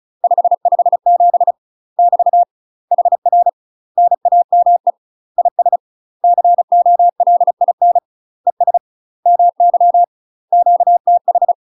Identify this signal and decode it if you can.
Morse code, heard as a keyed tone, about 36 wpm, 700 Hz. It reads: 557 <BT> HR NAME IS COLIN ES MY QTH
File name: UK MASTER QSOs TRAINING/35WPM